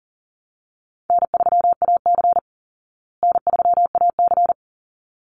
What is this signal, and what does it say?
Morse code, heard as a keyed tone, about 40 wpm, 700 Hz. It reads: N3AC N3AC
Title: Easy Call Signs Repeated 2x - 40wpm